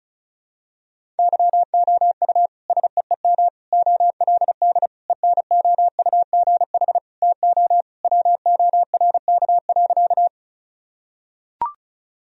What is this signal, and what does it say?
Morse code, heard as a keyed tone, about 36 wpm, 700 Hz. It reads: YOU SEEM OLD ENOUGH TO WORK.
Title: Sentences from Top 1000 Words - Review Part 1 - 35wpm